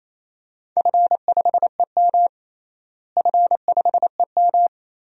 Morse code, heard as a keyed tone, about 28 wpm, 700 Hz. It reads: F5EM F5EM
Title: DXCC-100 Call Signs - 4 Characters - Repeated 3x - 28wpm